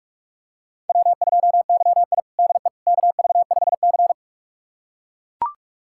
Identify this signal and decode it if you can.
Morse code, heard as a keyed tone, about 43 wpm, 700 Hz. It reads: W2YI DE KV5C
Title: USA Callsign DE Callsign - 4 Characters - 45wpm